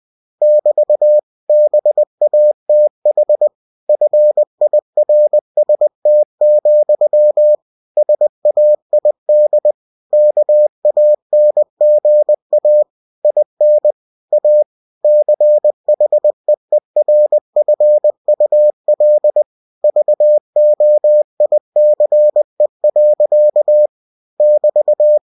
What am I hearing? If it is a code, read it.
Morse code, heard as a keyed tone, about 20 wpm, 600 Hz. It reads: <BT> BATH FIRST, SAID KANGA IN A CHEERFUL VOICE. <BT>
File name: cw-books / winnie-the-pooh / winnie-20wpm-page-0071